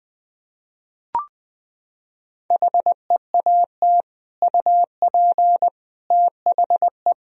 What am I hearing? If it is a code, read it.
Morse code, heard as a keyed tone, about 20 wpm, 700 Hz. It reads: HEAT UP THE
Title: Sentences from Top 1000 Words - Review Part 3 - 20wpm